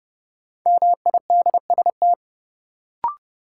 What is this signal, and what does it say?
Morse code, heard as a keyed tone, about 30 wpm, 700 Hz. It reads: MIDST
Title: Sets of 4 Words - New Word Drill - 30wpm